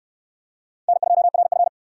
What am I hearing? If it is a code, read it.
Morse code, heard as a keyed tone, about 63 wpm, 700 Hz. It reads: N2RF